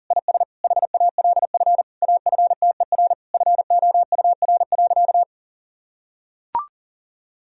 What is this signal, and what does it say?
Morse code, heard as a keyed tone, about 40 wpm, 700 Hz. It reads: IS HALF AFTER FOUR.
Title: Sentences from Top 1000 Words - Review Part 1 - 40wpm